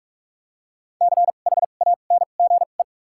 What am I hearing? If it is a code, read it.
Morse code, heard as a keyed tone, about 44 wpm, spaced as for 32 wpm, 700 Hz. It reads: CHANGE
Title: 2x Character Spacing - Top 200 Words - 45wpm